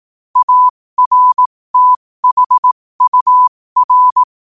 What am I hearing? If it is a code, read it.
Morse code, heard as a keyed tone, about 18 wpm, spaced as for 15 wpm, 1000 Hz. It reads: ARTHUR
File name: qotdfiles / 15wpm